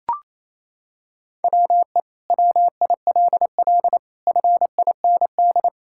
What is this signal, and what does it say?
Morse code, heard as a keyed tone, about 28 wpm, 700 Hz. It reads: WE WILL FIND
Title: Sentences from Top 1000 Words - Review Part 5 - 28wpm